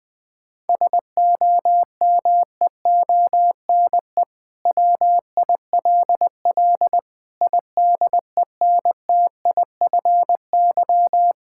Morse code, heard as a keyed tone, about 20 wpm, 700 Hz. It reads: SOMEONE WILL IDENTIFY
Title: Sentences from Top 1000 Words - Review Part 6 - 20wpm